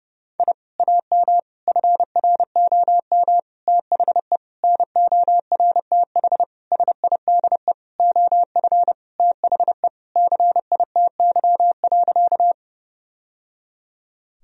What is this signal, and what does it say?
Morse code, heard as a keyed tone, about 31 wpm, 700 Hz. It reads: I AM FROM THE NORTH SIDE OF THE CITY.
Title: Sentences from Top 1000 Words - Review Part 1 - 30wpm